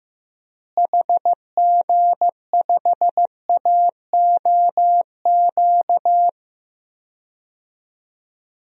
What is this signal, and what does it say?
Morse code, heard as a keyed tone, about 15 wpm, 700 Hz. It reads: HG5AOQ
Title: DXCC-100 Call Signs - 6 Characters - Repeated 3x - 15wpm